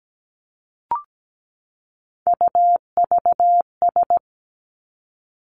Morse code, heard as a keyed tone, about 17 wpm, 700 Hz. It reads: UVS
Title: Three Challenging Letter-Number Combinations 17wpm